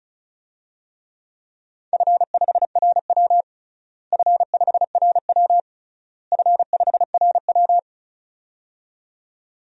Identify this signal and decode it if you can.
Morse code, heard as a keyed tone, about 36 wpm, 700 Hz. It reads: F5RW F5RW F5RW
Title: DXCC-100 Call Signs - 4 Characters - Repeated 3x - 35wpm